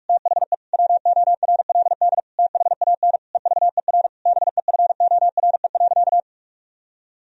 Morse code, heard as a keyed tone, about 46 wpm, 700 Hz. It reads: THE WORLD THAN EVER BEFORE.
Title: Sentences from Top 1000 Words - Review Part 3 - 45wpm